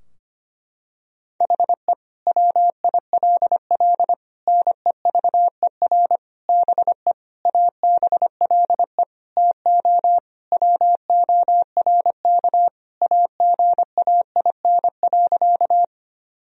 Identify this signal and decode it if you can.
Morse code, heard as a keyed tone, about 25 wpm, 700 Hz. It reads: HE WILL NEVER BE ABLE TO WORK AGAIN.
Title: Sentences from Top 1000 Words - Review Part 2 - 25wpm